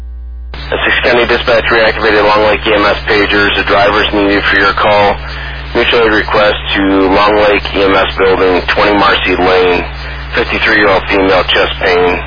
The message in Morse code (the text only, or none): none